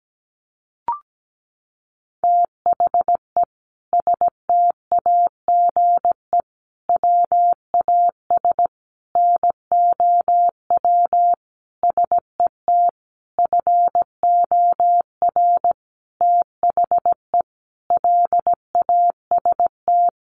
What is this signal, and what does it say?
Morse code, heard as a keyed tone, about 17 wpm, 700 Hz. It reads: THE STAGE WAS NOW SET FOR THE LAST
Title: Sentences from Top 1500 Words 17wpm